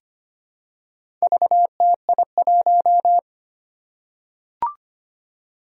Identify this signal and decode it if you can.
Morse code, heard as a keyed tone, about 25 wpm, 700 Hz. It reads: VTI1
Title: Four Challenging Letter-Number Combinations 25wpm